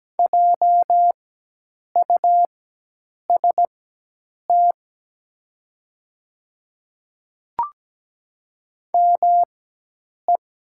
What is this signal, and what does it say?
Morse code, heard as a keyed tone, about 17 wpm, spaced as for 8 wpm, 700 Hz. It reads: JUST ME
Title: 4x Character Spacing - Top 500 Words - 17wpm